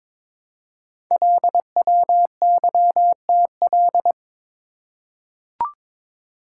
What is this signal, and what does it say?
Morse code, heard as a keyed tone, about 22 wpm, 700 Hz. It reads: LWYTL
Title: Code Groups - 5 Letters - 22wpm